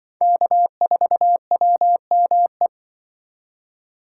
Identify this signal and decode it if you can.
Morse code, heard as a keyed tone, about 24 wpm, 700 Hz. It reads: K4WME
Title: Speed Racing - USA Call Signs 20wpm